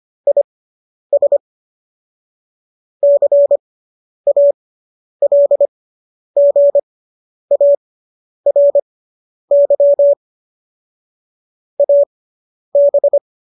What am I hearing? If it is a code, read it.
Morse code, heard as a keyed tone, about 25 wpm, spaced as for 10 wpm, 570 Hz. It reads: IS CALGARY AB